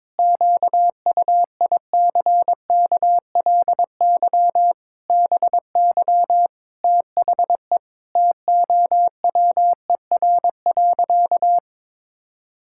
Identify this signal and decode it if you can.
Morse code, heard as a keyed tone, about 22 wpm, 700 Hz. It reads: QUICKLY BY THE TOWER.
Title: Pangram Challenge v2 - 22wpm